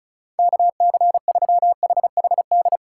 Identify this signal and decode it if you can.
Morse code, heard as a keyed tone, about 34 wpm, 700 Hz. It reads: KC3HHD
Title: USA Call Signs - 6 Characters - 6x Character Spacing - 35wpm